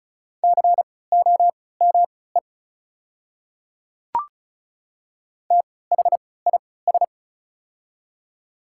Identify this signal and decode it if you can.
Morse code, heard as a keyed tone, about 36 wpm, spaced as for 20 wpm, 690 Hz. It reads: COME THIS
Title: 3x Character Spacing - Top 100 Words - 35wpm